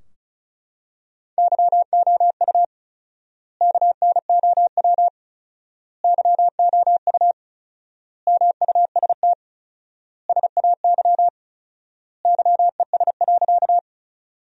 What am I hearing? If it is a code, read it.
Morse code, heard as a keyed tone, about 36 wpm, 700 Hz. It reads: YOU KNOW YOU MUST SAY YES.
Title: 4x Word-Spacing - Sentences from Top 500 Words - 35wpm